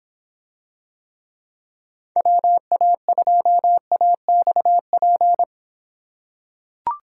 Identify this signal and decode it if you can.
Morse code, heard as a keyed tone, about 25 wpm, 700 Hz. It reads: WA2AXP